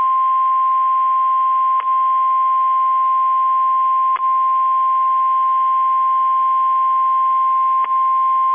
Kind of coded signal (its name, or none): Emergency Alert System (SAME) tones